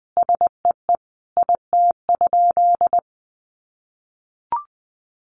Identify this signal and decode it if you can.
Morse code, heard as a keyed tone, about 20 wpm, 700 Hz. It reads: SEE IT?
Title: Sentences from Top 100 Words 20wpm